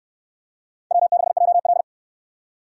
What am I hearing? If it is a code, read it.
Morse code, heard as a keyed tone, about 64 wpm, 700 Hz. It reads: W6JL